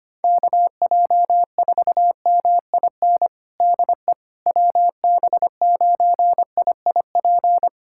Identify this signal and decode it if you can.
Morse code, heard as a keyed tone, about 25 wpm, 700 Hz. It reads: KJ4MIN DE WB9IIP
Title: USA Callsign DE Callsign - 6 Characters - 25wpm